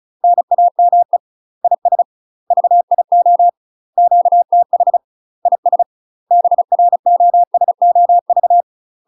Morse code, heard as a keyed tone, about 34 wpm, 700 Hz. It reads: NAME IS VIO QTH IS BROSOV
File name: UK MASTER QSOs TRAINING / 35WPM / UK MASTER QSOs 0042